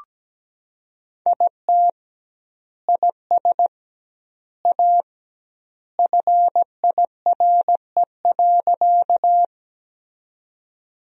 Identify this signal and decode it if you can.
Morse code, heard as a keyed tone, about 17 wpm, 700 Hz. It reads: IT IS A FIRE.
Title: 2x Word-Spacing - Sentences from Top 400 Words - 17wpm